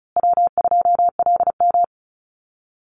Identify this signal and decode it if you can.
Morse code, heard as a keyed tone, about 34 wpm, 700 Hz. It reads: W2LM
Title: USA Call Signs Repeated 3x 35wpm